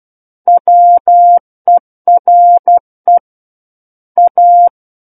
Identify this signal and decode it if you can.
Morse code, heard as a keyed tone, about 12 wpm, 700 Hz. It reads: WERE A
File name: Book One / 03-Chapter 3 - 12 WPM